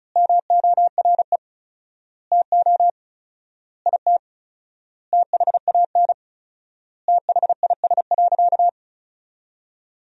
Morse code, heard as a keyed tone, about 36 wpm, 700 Hz. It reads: MORE TO IT THAN THIS.
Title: 4x Word-Spacing - Sentences from Top 100 Words - 35wpm